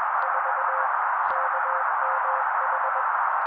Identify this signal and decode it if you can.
Morse code, heard as a keyed tone, about 21 wpm, 530 Hz. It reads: 4 KMH
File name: daily / 20wpm noise